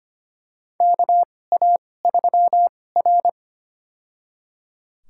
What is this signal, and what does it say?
Morse code, heard as a keyed tone, about 25 wpm, spaced as for 18 wpm, 700 Hz. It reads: KA3R
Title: USA Call Signs - 2x Character Spacing - 25wpm